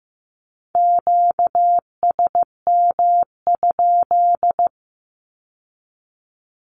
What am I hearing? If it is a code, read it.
Morse code, heard as a keyed tone, about 15 wpm, 700 Hz. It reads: QSM?
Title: Q-Codes with Spoken Meaning - 15wpm